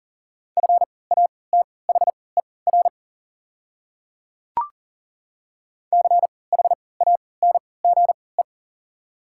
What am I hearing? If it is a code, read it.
Morse code, heard as a keyed tone, about 40 wpm, spaced as for 23 wpm, 700 Hz. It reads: FATHER CHANGE
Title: 3x Character Spacing - Top 200 Words - 40wpm